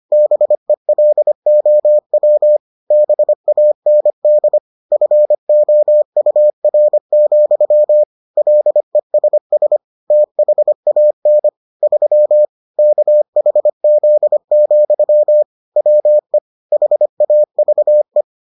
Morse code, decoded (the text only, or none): BELOW BAND FOUR, LESS THAN 3 KHZ, WE HAVE